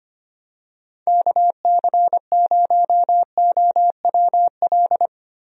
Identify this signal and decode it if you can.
Morse code, heard as a keyed tone, about 25 wpm, 700 Hz. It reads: KC0OWL
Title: USA Call Signs - 6 Characters - Repeated 3x - 25wpm